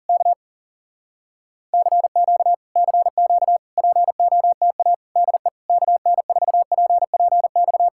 Morse code, heard as a keyed tone, about 40 wpm, 700 Hz. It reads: K CQ CQ POTA DE KN4PPX